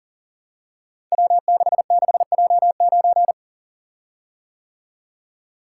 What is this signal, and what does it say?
Morse code, heard as a keyed tone, about 40 wpm, 700 Hz. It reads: W66J9
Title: Code Groups - 5 Letter:Numbers - 40wpm